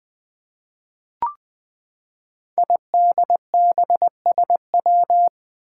Morse code, heard as a keyed tone, about 20 wpm, 700 Hz. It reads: IDBSW
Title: Five Challenging Letter-Number Combinations 20wpm